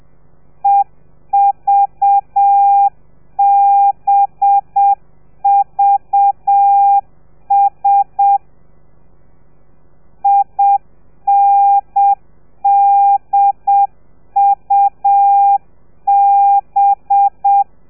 Morse code, heard as a keyed tone, about 7 wpm, 800 Hz. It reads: EVBVS INDUB
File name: telegrafia / i2tfj / Lez4 30